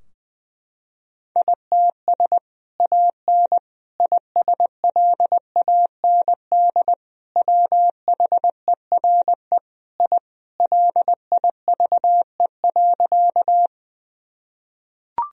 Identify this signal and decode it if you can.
Morse code, heard as a keyed tone, about 20 wpm, 700 Hz. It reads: ITS AN ISLAND WHERE I LIVE.
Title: Sentences from Top 1000 Words - Review Part 3 - 20wpm